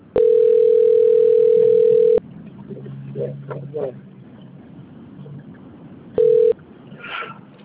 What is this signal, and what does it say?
Ringback tone (North America), 440 + 480 Hz.
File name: out- - - - - .72284